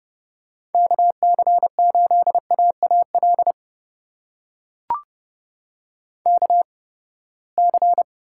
Morse code, heard as a keyed tone, about 30 wpm, spaced as for 21 wpm, 700 Hz. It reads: KC8AAL KC
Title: USA Call Signs - 6 Characters - 8x Character Spacing - 30wpm